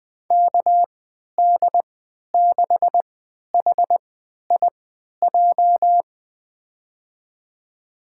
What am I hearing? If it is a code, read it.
Morse code, heard as a keyed tone, about 20 wpm, spaced as for 11 wpm, 700 Hz. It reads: KD6HIJ
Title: USA Call Signs - 3x Character Spacing - 20wpm